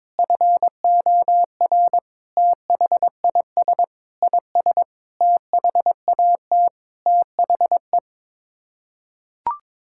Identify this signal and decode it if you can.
Morse code, heard as a keyed tone, about 21 wpm, 700 Hz. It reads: FOR THIS IS THAT THE
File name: practice / 22wpm / Sets of 7 Words - 22wpm